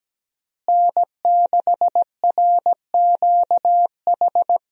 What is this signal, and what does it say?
Morse code, heard as a keyed tone, about 17 wpm, 700 Hz. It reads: N6RQH
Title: USA Call Signs - 5 Characters - Repeated 3x - 17wpm